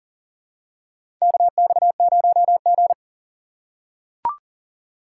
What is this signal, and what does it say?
Morse code, heard as a keyed tone, about 40 wpm, 700 Hz. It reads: KX0G
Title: USA Call Signs - 4 Characters - 3x Character Spacing - 40wpm